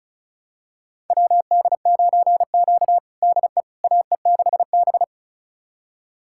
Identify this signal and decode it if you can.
Morse code, heard as a keyed tone, about 36 wpm, 700 Hz. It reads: WD9Q DE AE6B
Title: USA Callsign DE Callsign - 4 Characters - 35wpm